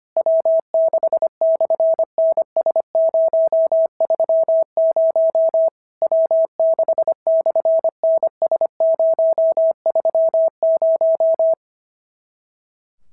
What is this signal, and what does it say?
Morse code, heard as a keyed tone, about 25 wpm, 650 Hz. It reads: W6/NS030 W6/NS030
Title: SOTA QSO Practice 25wpm